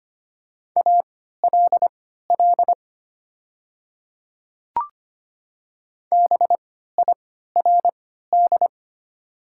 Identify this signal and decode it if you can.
Morse code, heard as a keyed tone, about 25 wpm, spaced as for 14 wpm, 700 Hz. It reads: ALL BIRD
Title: 3x Character Spacing - Top 500 Words - 25wpm